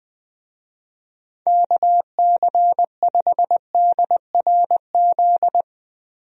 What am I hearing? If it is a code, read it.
Morse code, heard as a keyed tone, about 20 wpm, 700 Hz. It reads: KC5DRZ
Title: USA Call Signs - 6 Characters - Repeated 2x - 20wpm